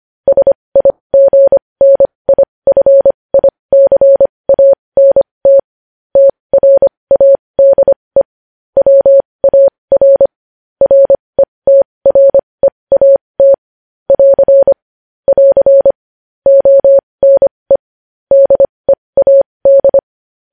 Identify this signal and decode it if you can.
Morse code, heard as a keyed tone, about 25 wpm, spaced as for 20 wpm, 550 Hz. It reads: SIGNIFICANT TRADE WAR RETREAT <AR> <AR> ONE DEAD
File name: podcast / News headlines-1x-25x20